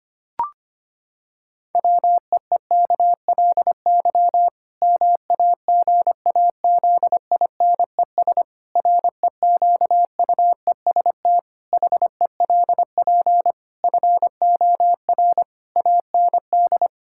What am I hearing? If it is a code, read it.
Morse code, heard as a keyed tone, about 25 wpm, 700 Hz. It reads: WEEKLY MAGAZINES REQUEST HELP FOR AND
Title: Pangram Challenge v2 - 25wpm